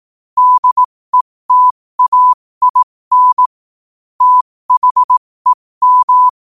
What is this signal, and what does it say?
Morse code, heard as a keyed tone, about 18 wpm, spaced as for 15 wpm, 1000 Hz. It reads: DETAIN THEM
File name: qotdfiles / 15wpm